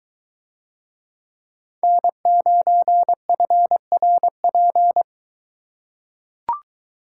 Morse code, heard as a keyed tone, about 21 wpm, 700 Hz. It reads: N9FRP